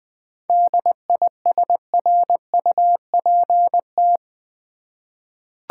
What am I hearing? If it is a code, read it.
Morse code, heard as a keyed tone, about 20 wpm, 700 Hz. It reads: DISRUPT
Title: 7 Letter Words - First 3 Letters Given - 20wpm